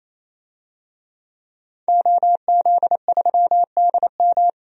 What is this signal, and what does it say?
Morse code, heard as a keyed tone, about 27 wpm, 700 Hz. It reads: OZ3DM
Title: DXCC-100 Call Signs - 5 Characters - 28wpm